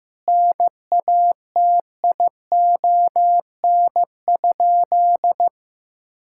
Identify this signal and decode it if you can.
Morse code, heard as a keyed tone, about 15 wpm, 700 Hz. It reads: NATION?
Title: Sentences from Top 1000 Words - Review Part 6 - 15wpm